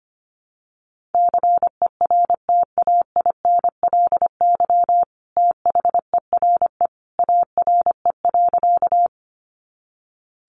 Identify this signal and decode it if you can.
Morse code, heard as a keyed tone, about 25 wpm, 700 Hz. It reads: CERTAINLY THERE ARE.
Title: Sentences from Top 900 Words 25wpm